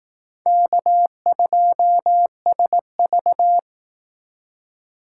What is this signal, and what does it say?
Morse code, heard as a keyed tone, about 19 wpm, 700 Hz. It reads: K2SV